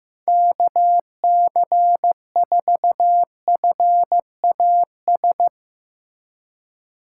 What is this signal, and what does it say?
Morse code, heard as a keyed tone, about 15 wpm, 700 Hz. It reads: KC4FAS
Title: USA Call Signs - 4x Character Spacing - 15wpm